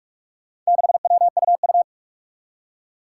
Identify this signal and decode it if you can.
Morse code, heard as a keyed tone, about 44 wpm, 700 Hz. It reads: 6WUU
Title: Four Challenging Letter-Number Combinations 45wpm